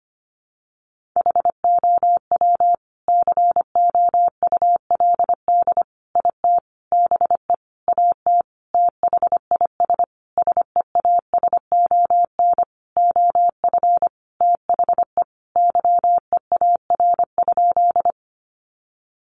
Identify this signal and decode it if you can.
Morse code, heard as a keyed tone, about 25 wpm, 700 Hz. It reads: HOW COULD IT BE AT THIS SEASON OF THE YEAR?
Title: Sentences from Top 1400 Words 25wpm